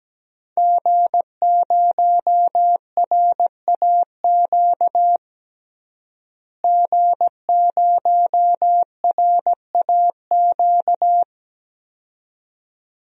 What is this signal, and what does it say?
Morse code, heard as a keyed tone, about 17 wpm, 700 Hz. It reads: G0RAQ G0RAQ
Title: DXCC-100 Call Signs - 5 Characters - Repeated 2x - 17wpm